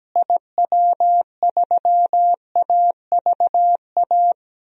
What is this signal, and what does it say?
Morse code, heard as a keyed tone, about 17 wpm, 700 Hz. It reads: IW3AVA
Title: DXCC-100 Call Signs - 6 Characters - Repeated 2x - 17wpm